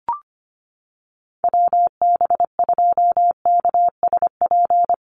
Morse code, heard as a keyed tone, about 25 wpm, 700 Hz. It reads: WB2KSP
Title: USA Call Signs Repeated 3x 25wpm